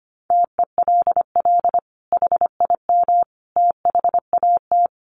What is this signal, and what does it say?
Morse code, heard as a keyed tone, about 25 wpm, 700 Hz. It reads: TELL HIM THAT
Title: Sentences from Top 1500 Words 25wpm